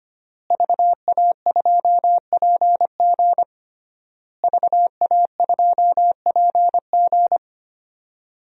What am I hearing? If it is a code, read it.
Morse code, heard as a keyed tone, about 25 wpm, 700 Hz. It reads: VA2PG VA2PG
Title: DXCC-100 Call Signs - 5 Characters - Repeated 3x - 25wpm